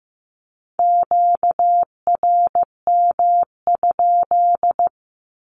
Morse code, heard as a keyed tone, about 15 wpm, 700 Hz. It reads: QRM?
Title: Q-Codes 15wpm